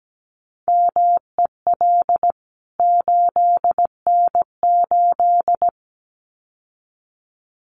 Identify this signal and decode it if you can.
Morse code, heard as a keyed tone, about 17 wpm, 700 Hz. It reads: MEL 8N8
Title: CWT Exchange 17wpm